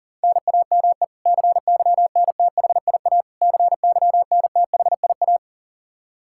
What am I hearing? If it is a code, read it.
Morse code, heard as a keyed tone, about 40 wpm, 700 Hz. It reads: NAME CYNTHIA CYNTHIA
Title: K1USN SST QSO Practice 40wpm